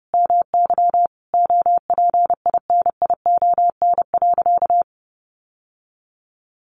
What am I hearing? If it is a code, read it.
Morse code, heard as a keyed tone, about 30 wpm, 700 Hz. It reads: MY OPINION.
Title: Sentences from Top 1200 Words 30wpm